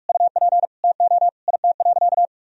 Morse code, heard as a keyed tone, about 46 wpm, 700 Hz. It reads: UP TO IT.
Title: Sentences from Top 1000 Words - Review Part 2 - 45wpm